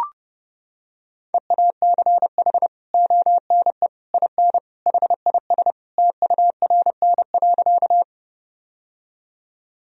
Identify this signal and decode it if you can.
Morse code, heard as a keyed tone, about 32 wpm, 700 Hz. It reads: EACH ONE IN HIS TURN.
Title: Sentences from Top 1000 Words - Review Part 1 - 30wpm